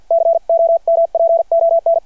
Morse code, heard as a keyed tone, about 51 wpm, spaced as for 44 wpm, 670 Hz. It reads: COMPON
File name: archived / CRSnet